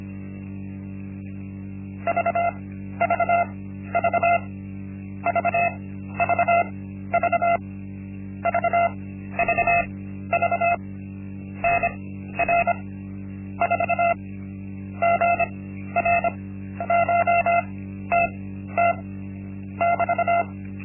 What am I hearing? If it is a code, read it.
Morse code, heard as a keyed tone, about 25 wpm, spaced as for 12 wpm, 690 Hz, with noise. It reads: VVV VVV VVV NR 4 GRCHTT <BT>
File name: Linksmakalnis-1